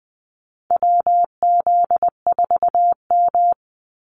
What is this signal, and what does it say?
Morse code, heard as a keyed tone, about 20 wpm, 700 Hz. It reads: WZ4M
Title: NA SOTA Activator Callsigns 20wpm